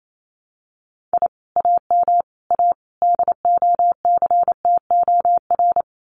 Morse code, heard as a keyed tone, about 28 wpm, 700 Hz. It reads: I AM A DOCTOR
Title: Common QSO Phrases Encore 28wpm